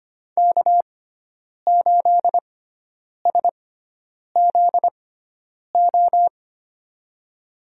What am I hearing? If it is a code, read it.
Morse code, heard as a keyed tone, about 25 wpm, spaced as for 9 wpm, 700 Hz. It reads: K8SZO
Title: USA Call Signs - 5 Characters - 6x Character Spacing - 25wpm